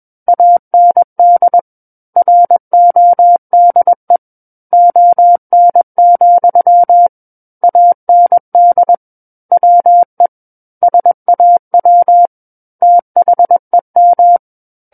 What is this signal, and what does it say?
Morse code, heard as a keyed tone, about 21 wpm, 700 Hz. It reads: AND RODE ON, AND WE SAW THEM
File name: Book One / 12-Chapter 12 - 21 WPM